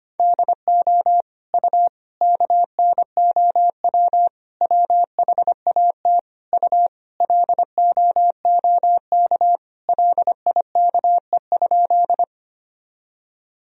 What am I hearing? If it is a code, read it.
Morse code, heard as a keyed tone, about 25 wpm, 700 Hz. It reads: DO U KNOW WHAT U LOOK LIKE?
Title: Sentences from Top 100 Words and Abbreviations 25wpm